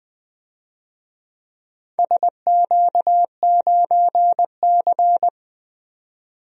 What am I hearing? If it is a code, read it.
Morse code, heard as a keyed tone, about 20 wpm, 700 Hz. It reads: SQ9C
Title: DXCC-100 Call Signs - 4 Characters - 20wpm